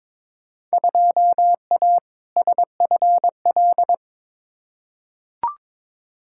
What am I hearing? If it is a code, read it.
Morse code, heard as a keyed tone, about 22 wpm, 700 Hz. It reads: 2A SFL
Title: Field Day Exchanges Encore - 22wpm